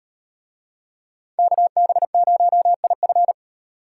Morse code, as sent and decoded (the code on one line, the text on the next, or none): -.- -... ----- .. ..-.
KB0IF